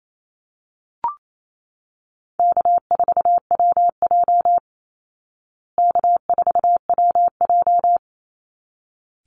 Morse code, heard as a keyed tone, about 28 wpm, 700 Hz. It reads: K4WJ K4WJ
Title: Easy Call Signs Repeated 2x - 28wpm